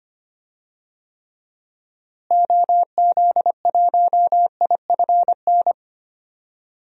Morse code, heard as a keyed tone, about 25 wpm, 700 Hz. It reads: OZ1IFN
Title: DXCC-100 Call Signs - 6 Characters - Repeated 3x - 25wpm